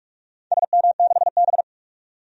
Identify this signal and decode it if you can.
Morse code, heard as a keyed tone, about 41 wpm, 700 Hz. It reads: SM6B
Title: DXCC-100 Call Signs - 4 Characters - 45wpm